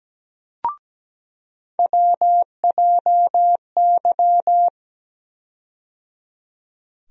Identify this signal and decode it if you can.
Morse code, heard as a keyed tone, about 17 wpm, 700 Hz. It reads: WJY
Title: USA Call Sign Postfixes - 3 Letters 17wpm